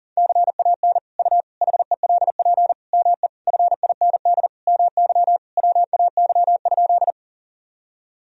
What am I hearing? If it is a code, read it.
Morse code, heard as a keyed tone, about 40 wpm, 700 Hz. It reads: CAN U HELP ME FIND MY WAY?
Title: Sentences from Top 200 Words and Abbreviations 40wpm